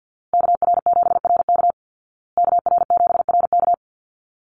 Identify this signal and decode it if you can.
Morse code, heard as a keyed tone, about 50 wpm, 700 Hz. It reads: KR7RK KR7RK
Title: Call Signs Repeated 2x - 50wpm